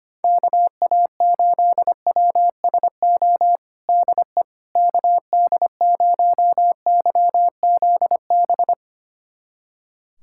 Morse code, heard as a keyed tone, about 25 wpm, 700 Hz. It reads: KA8WSO DE KD0YZB
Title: USA Callsign DE Callsign - 6 Characters - 25wpm